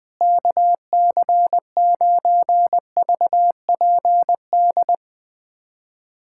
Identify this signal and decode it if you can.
Morse code, heard as a keyed tone, about 20 wpm, 700 Hz. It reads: KC9VPD
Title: USA Callsign DE Callsign - 6 Characters - 20wpm